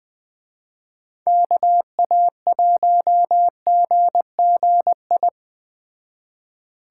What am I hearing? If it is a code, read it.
Morse code, heard as a keyed tone, about 20 wpm, 700 Hz. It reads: KA1GGI